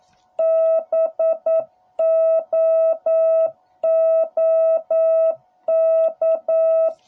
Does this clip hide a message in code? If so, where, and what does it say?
Yes: Morse code from 0.4 to 6.9 s, heard as a keyed tone, about 9 wpm, 660 Hz. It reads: BOOK